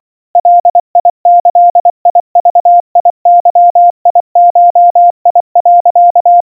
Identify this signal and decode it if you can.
Morse code (Japanese Wabun code), heard as a keyed tone, about 24 wpm, 700 Hz. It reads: ガギグゲゴ、
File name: iroha 80 cpm 700 Hz Crispy